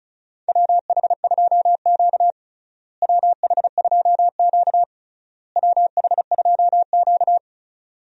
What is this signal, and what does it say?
Morse code, heard as a keyed tone, about 34 wpm, 700 Hz. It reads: WH2Q WH2Q WH2Q
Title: USA Call Signs - 4 Characters - Repeated 3x - 35wpm